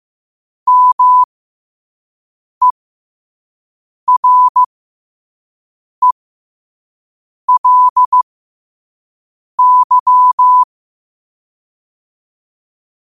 Morse code, heard as a keyed tone, about 15 wpm, spaced as for 5 wpm, 1000 Hz. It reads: MERELY
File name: qotdfiles / 5wpm